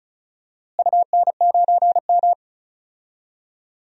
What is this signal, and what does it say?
Morse code, heard as a keyed tone, about 34 wpm, 700 Hz. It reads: UN9M
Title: DXCC-100 Call Signs - 4 Characters - Repeated 2x - 35wpm